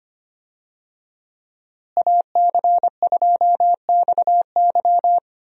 Morse code, heard as a keyed tone, about 25 wpm, 700 Hz. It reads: AC2XY
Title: USA Call Signs - 5 Characters - Repeated 2x - 25wpm